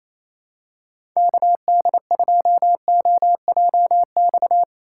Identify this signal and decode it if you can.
Morse code, heard as a keyed tone, about 28 wpm, 700 Hz. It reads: KD2OJX
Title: USA Call Signs - 3x Character Spacing - 28wpm